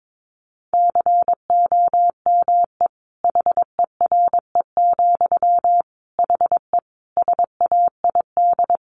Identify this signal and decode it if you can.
Morse code, heard as a keyed tone, about 22 wpm, 700 Hz. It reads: COME HERE, HE SAID
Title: Sentences from Top 1500 Words 22wpm